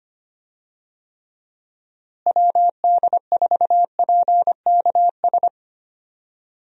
Morse code, heard as a keyed tone, about 25 wpm, 700 Hz. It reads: WD4PKS
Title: USA Call Signs - 6 Characters - Repeated 3x - 25wpm